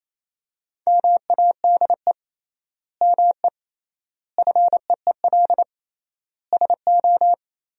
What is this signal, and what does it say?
Morse code, heard as a keyed tone, about 28 wpm, 700 Hz. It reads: MADE ME FEEL SO
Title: 3x Word-Spacing - Sentences from Top 400 Words - 28wpm